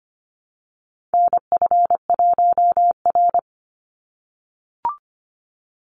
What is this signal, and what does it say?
Morse code, heard as a keyed tone, about 25 wpm, 700 Hz. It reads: NF1R
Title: NA SOTA Chaser Callsigns 25wpm